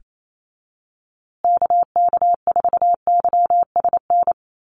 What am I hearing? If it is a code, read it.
Morse code, heard as a keyed tone, about 28 wpm, 700 Hz. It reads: KK4YSN
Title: USA Call Signs Repeated 2x 28wpm